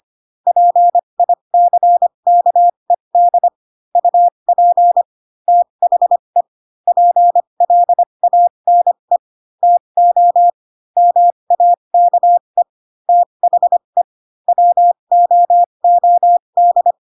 Morse code, heard as a keyed tone, about 25 wpm, spaced as for 22 wpm, 700 Hz. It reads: PICKED UP THE PLANE TO MAKE THE WOOD